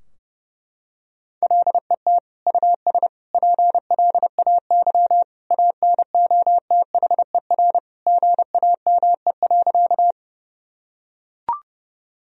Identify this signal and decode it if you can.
Morse code, heard as a keyed tone, about 30 wpm, 700 Hz. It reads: LET US PLAY ANOTHER GAME.
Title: Sentences from Top 1000 Words - Review Part 3 - 30wpm